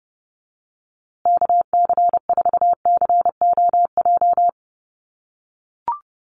Morse code, heard as a keyed tone, about 30 wpm, 700 Hz. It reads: KC4COJ
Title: USA Call Signs Repeated 2x 30wpm